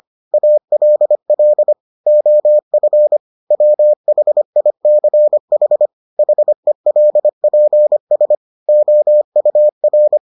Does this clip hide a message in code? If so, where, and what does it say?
Yes: Morse code from 0.3 to 10.2 s, heard as a keyed tone, about 25 wpm, 600 Hz. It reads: ALL OF WHICH HELPS OUR